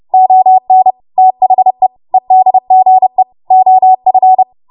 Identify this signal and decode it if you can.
Morse code, heard as a keyed tone, about 30 wpm, 750 Hz. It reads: ON THE EDGE OF
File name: Morse / 30WPM